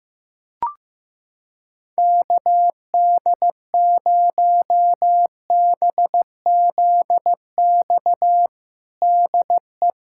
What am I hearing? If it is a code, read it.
Morse code, heard as a keyed tone, about 15 wpm, 700 Hz. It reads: KD0BZX DE
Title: USA Callsign DE Callsign - 6 Characters - 15wpm